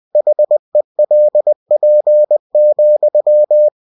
Morse code, heard as a keyed tone, about 20 wpm, 600 Hz. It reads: HELP,